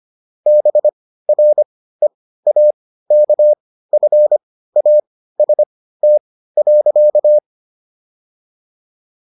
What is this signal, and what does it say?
Morse code, heard as a keyed tone, about 25 wpm, spaced as for 15 wpm, 600 Hz. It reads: BREAKFAST.